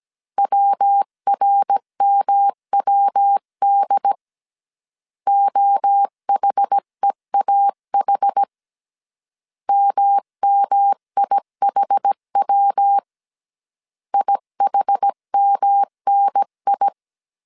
Morse code, heard as a keyed tone, about 17 wpm, 800 Hz. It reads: WRMWD OHEAH MMIHW IHMNI